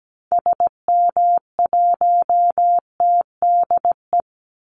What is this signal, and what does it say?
Morse code, heard as a keyed tone, about 17 wpm, 700 Hz. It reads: SM1TDE
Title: Call Signs Repeated 2x - 17wpm